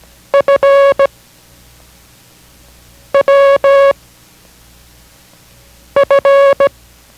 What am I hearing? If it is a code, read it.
Morse code, heard as a keyed tone, about 17 wpm, 560 Hz. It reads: F W F